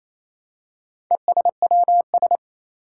Morse code, heard as a keyed tone, about 27 wpm, 700 Hz. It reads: ESWS
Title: Four Challenging Letter-Number Combinations 28wpm